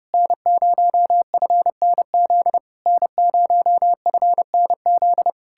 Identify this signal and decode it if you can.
Morse code, heard as a keyed tone, about 33 wpm, 700 Hz. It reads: N0FNZ N0FNZ